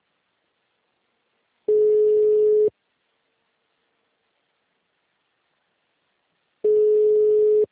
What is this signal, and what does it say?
Ringback tone (Europe, 425 Hz).